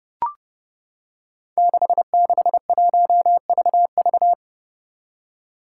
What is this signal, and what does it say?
Morse code, heard as a keyed tone, about 30 wpm, 700 Hz. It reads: 661VV
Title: Five Challenging Letter-Number Combinations 30wpm